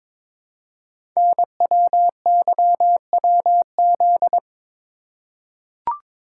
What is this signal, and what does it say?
Morse code, heard as a keyed tone, about 23 wpm, 700 Hz. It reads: NWYWZ
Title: Code Groups - 5 Letters - 22wpm